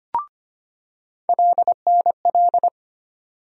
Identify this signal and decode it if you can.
Morse code, heard as a keyed tone, about 25 wpm, 700 Hz. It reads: LNL
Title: USA Call Sign Postfixes - 3 Letters 25wpm